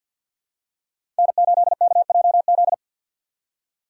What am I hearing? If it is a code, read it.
Morse code, heard as a keyed tone, about 52 wpm, 700 Hz. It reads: N8KJZ
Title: Code Groups - 5 Letter:Numbers - 50wpm